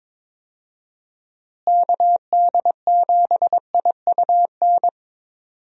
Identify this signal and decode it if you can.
Morse code, heard as a keyed tone, about 22 wpm, 700 Hz. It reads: KD7IUN
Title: USA Call Signs - 6 Characters - Repeated 3x - 22wpm